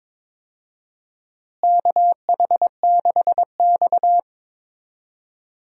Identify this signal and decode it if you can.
Morse code, heard as a keyed tone, about 22 wpm, 700 Hz. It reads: KH6X
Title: DXCC-100 Call Signs - 4 Characters - Repeated 2x - 22wpm